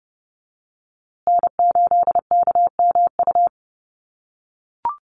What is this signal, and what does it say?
Morse code, heard as a keyed tone, about 30 wpm, 700 Hz. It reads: N8KMU
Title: USA Call Signs Repeated 2x 30wpm